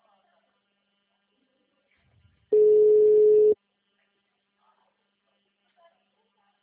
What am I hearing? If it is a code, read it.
Ringback tone (Europe, 425 Hz).